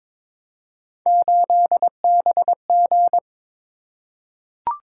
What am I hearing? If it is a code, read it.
Morse code, heard as a keyed tone, about 22 wpm, 700 Hz. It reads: 8BG
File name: practice / 22wpm / Three Letter-Number Combinations - 22wpm